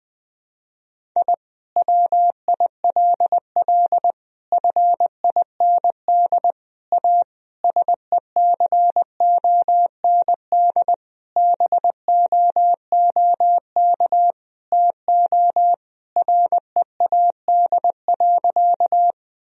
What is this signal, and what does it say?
Morse code, heard as a keyed tone, about 20 wpm, 700 Hz. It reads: I WILL FIND A SECOND BOOK TO READ.
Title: Sentences from Top 1000 Words - Review Part 1 - 20wpm